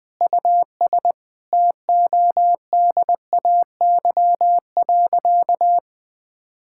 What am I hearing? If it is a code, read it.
Morse code, heard as a keyed tone, about 21 wpm, 700 Hz. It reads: US TODAY.
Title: Sentences with 5-Word Phrases - 20wpm